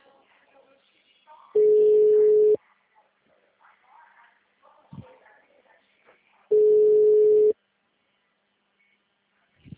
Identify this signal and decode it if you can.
Ringback tone (Europe, 425 Hz).